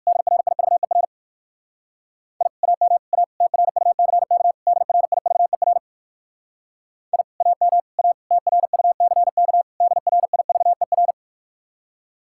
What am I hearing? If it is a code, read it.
Morse code, heard as a keyed tone, about 59 wpm, 700 Hz. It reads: DRIVER I AM A TRUCK DRIVER I AM A TRUCK DRIVER